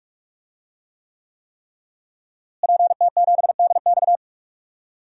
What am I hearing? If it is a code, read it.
Morse code, heard as a keyed tone, about 41 wpm, 700 Hz. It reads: PT7DX
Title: DXCC-100 Call Signs - 5 Characters - Repeated 3x - 45wpm